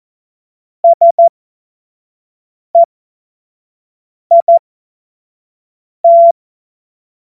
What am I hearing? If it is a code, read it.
Morse code, heard as a keyed tone, about 14 wpm, spaced as for 5 wpm, 680 Hz. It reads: SEIT
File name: Morsekurs / Witze / Witz 15. Februar 2022 25 F70